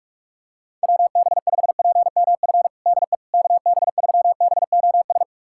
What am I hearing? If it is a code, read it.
Morse code, heard as a keyed tone, about 46 wpm, 690 Hz. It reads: WB5PMF DE KB3BOS
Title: USA Callsign DE Callsign 45wpm